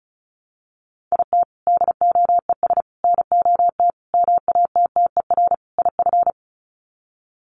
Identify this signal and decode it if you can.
Morse code, heard as a keyed tone, about 34 wpm, 700 Hz. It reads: IT DOES NOT MATTER IF
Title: Sets of 5 Words Encore 35wpm